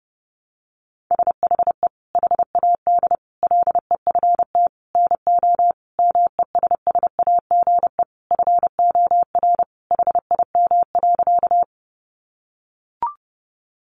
Morse code, heard as a keyed tone, about 30 wpm, 700 Hz. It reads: SHE HAD LEFT NO MESSAGE FOR HIM.
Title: Sentences from Top 1100 Words 30wpm